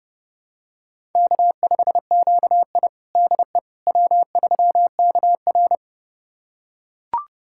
Morse code, heard as a keyed tone, about 30 wpm, 700 Hz. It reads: K5QI DE W3KR
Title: USA Callsign DE Callsign - 4 Characters - 30wpm